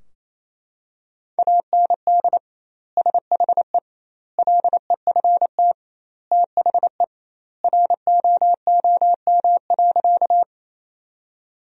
Morse code, heard as a keyed tone, about 28 wpm, 700 Hz. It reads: AND SHE LEFT THE ROOM.
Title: 2x Word-Spacing - Sentences from Top 300 Words - 28wpm